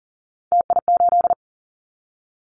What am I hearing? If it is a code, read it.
Morse code, heard as a keyed tone, about 40 wpm, 700 Hz. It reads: TI8
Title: Callsign Prefixes for Top 100 DXCC - 40wpm